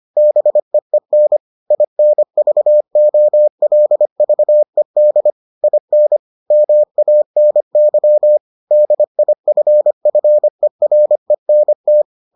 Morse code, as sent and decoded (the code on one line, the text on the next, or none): -... . -. / .. -. ...- --- .-.. ...- . -.. / .. -. / -- .- -. -.-- / -.. .. ..-. ..-. .-. -. -
BEEN INVOLVED IN MANY DIFFERENT